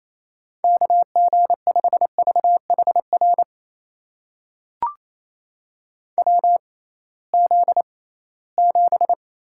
Morse code, heard as a keyed tone, about 28 wpm, spaced as for 18 wpm, 700 Hz. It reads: KG5VHR WZ7
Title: USA Call Signs - 6x Character Spacing - 28wpm